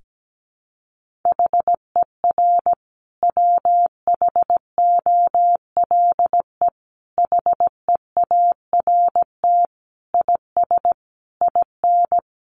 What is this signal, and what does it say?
Morse code, heard as a keyed tone, about 17 wpm, 700 Hz. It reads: HER WHOLE HEART IS IN
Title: Sentences from Top 1500 Words 17wpm